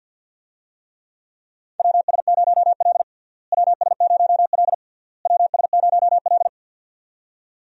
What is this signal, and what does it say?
Morse code, heard as a keyed tone, about 47 wpm, 700 Hz. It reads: WS0L WS0L WS0L
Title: USA Call Signs - 4 Characters - Repeated 3x - 50wpm